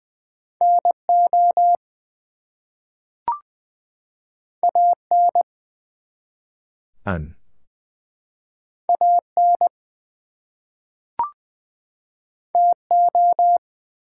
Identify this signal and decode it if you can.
Morse code, heard as a keyed tone, about 20 wpm, 700 Hz. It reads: NO AN AN TO
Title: Two Letter Words - 20wpm